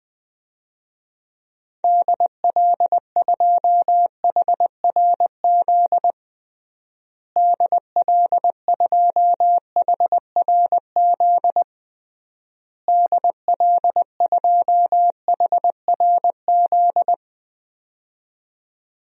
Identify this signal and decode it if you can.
Morse code, heard as a keyed tone, about 20 wpm, 700 Hz. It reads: DL2HRZ DL2HRZ DL2HRZ
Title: DXCC-100 Call Signs - 6 Characters - Repeated 3x - 20wpm